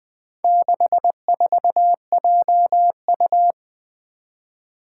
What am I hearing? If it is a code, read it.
Morse code, heard as a keyed tone, about 21 wpm, 700 Hz. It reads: 64JU
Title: Four Challenging Letter-Number Combinations 20wpm